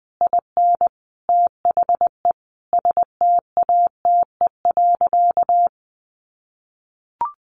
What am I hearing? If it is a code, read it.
Morse code, heard as a keyed tone, about 20 wpm, 700 Hz. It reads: IN THE STATE.
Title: Sentences from Top 1300 Words 20wpm